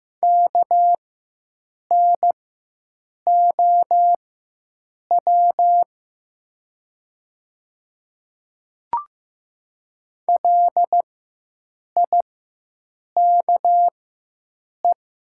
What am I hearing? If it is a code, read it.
Morse code, heard as a keyed tone, about 15 wpm, spaced as for 7 wpm, 700 Hz. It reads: KNOW LIKE
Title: 4x Character Spacing - Top 200 Words - 15wpm